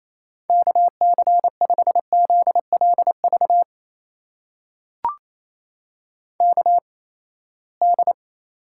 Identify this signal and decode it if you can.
Morse code, heard as a keyed tone, about 28 wpm, 700 Hz. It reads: K C 5 Z L V K D
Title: USA Call Signs - 6 Characters - 8x Character Spacing - 28wpm